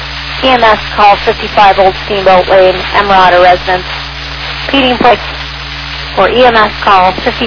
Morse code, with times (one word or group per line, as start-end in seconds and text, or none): none